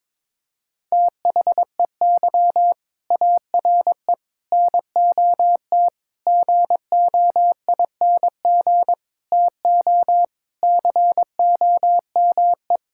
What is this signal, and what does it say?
Morse code, heard as a keyed tone, about 22 wpm, 700 Hz. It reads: THEY ARE NOT GOING TO COME
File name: practice / 22wpm / Sentences with 4-Word Phrases - 22wpm